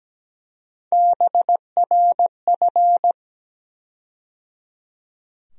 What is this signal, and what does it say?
Morse code, heard as a keyed tone, about 17 wpm, 700 Hz. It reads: BRF
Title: USA Call Sign Postfixes - 3 Letters 17wpm